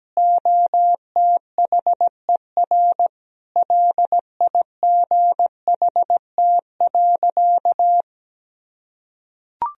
Morse code, heard as a keyed tone, about 17 wpm, 700 Hz. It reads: OTHER LIGHT.
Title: Sentences from Top 1000 Words - Review Part 1 - 17wpm